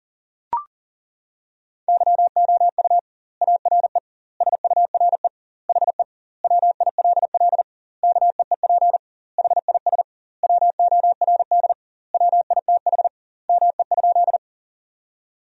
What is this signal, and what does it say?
Morse code, heard as a keyed tone, about 40 wpm, 700 Hz. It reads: YOU ARE SURE HE WILL KEEP HIS WORD WITH ME?
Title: 2x Word-Spacing - Sentences from Top 300 Words - 40wpm